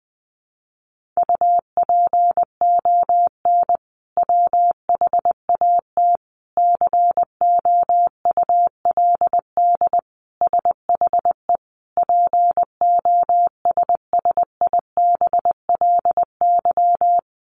Morse code, heard as a keyed tone, about 19 wpm, 700 Hz. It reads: UPON WHAT COULD SHE POSSIBLY
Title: Sentences from Top 1500 Words 20wpm